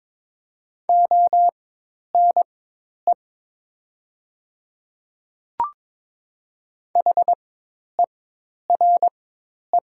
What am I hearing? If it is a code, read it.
Morse code, heard as a keyed tone, about 22 wpm, spaced as for 10 wpm, 700 Hz. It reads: ONE HERE
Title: 4x Character Spacing - Top 200 Words - 22wpm